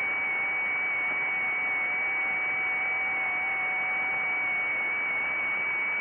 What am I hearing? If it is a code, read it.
FT8.
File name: IC-7100